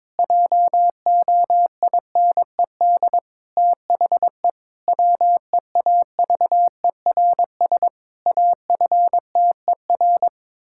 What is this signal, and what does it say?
Morse code, heard as a keyed tone, about 22 wpm, 700 Hz. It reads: JOINED THE WEAVERS AFTER
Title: Pangram Challenge v2 - 22wpm